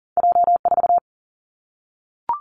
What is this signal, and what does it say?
Morse code, heard as a keyed tone, about 42 wpm, 700 Hz. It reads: J 4
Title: Two Challenging Letter-Number Combinations 40wpm